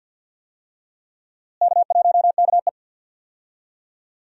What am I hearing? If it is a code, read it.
Morse code, heard as a keyed tone, about 47 wpm, 700 Hz. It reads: K1KE
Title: USA Call Signs - 4 Characters - Repeated 3x - 50wpm